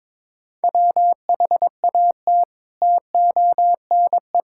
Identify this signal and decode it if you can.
Morse code, heard as a keyed tone, about 22 wpm, 700 Hz. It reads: WHAT TONE
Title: Sentences from Top 1600 Words 22wpm